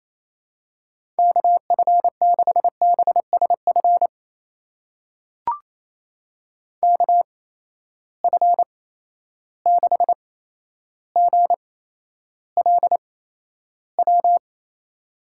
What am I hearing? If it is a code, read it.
Morse code, heard as a keyed tone, about 28 wpm, spaced as for 12 wpm, 700 Hz. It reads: KF6BSF KF6GLW
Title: USA Call Signs - 6 Characters - 8x Character Spacing - 28wpm